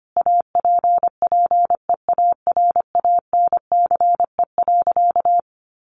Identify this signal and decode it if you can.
Morse code, heard as a keyed tone, about 25 wpm, 700 Hz. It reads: APPEARANCE.
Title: Sentences from Top 1100 Words 25wpm